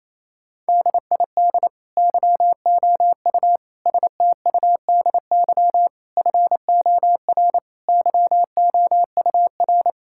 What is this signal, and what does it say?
Morse code, heard as a keyed tone, about 28 wpm, 700 Hz. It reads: DID YOU STUDY FOR YOUR
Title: Sentences from Top 1000 Words - Review Part 6 - 28wpm